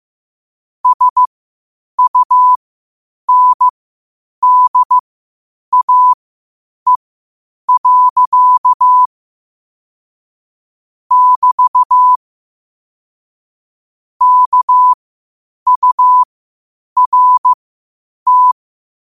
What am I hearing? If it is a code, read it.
Morse code, heard as a keyed tone, about 15 wpm, spaced as for 8 wpm, 1000 Hz. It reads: SUNDAE. <BT> KURT
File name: qotdfiles / 8wpm